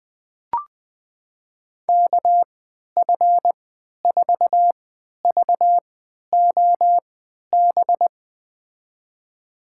Morse code, heard as a keyed tone, about 21 wpm, spaced as for 11 wpm, 700 Hz. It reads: KF4VOB
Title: USA Call Signs - 6 Characters - 3x Character Spacing - 20wpm